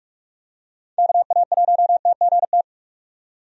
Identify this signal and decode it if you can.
Morse code, heard as a keyed tone, about 43 wpm, 700 Hz. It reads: KA1TGT
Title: USA Call Signs - 6 Characters - 45wpm